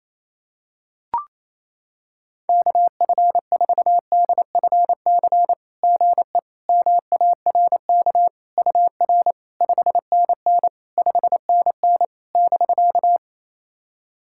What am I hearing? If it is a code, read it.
Morse code, heard as a keyed tone, about 28 wpm, 700 Hz. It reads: KF4DFC GE MARK UR 5NN 5NN <BK>
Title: SOTA QSO Practice 28wpm